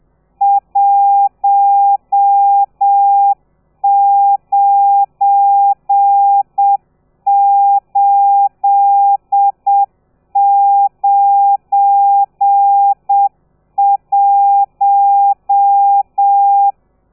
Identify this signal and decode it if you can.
Morse code, heard as a keyed tone, about 7 wpm, 800 Hz. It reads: 19891
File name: telegrafia / i2tfj / Lez12 30